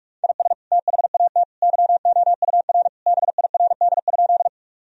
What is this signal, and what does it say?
Morse code, heard as a keyed tone, about 46 wpm, 700 Hz. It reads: IS THAT YOUR BIRD?
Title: Sentences from Top 1000 Words - Review Part 2 - 45wpm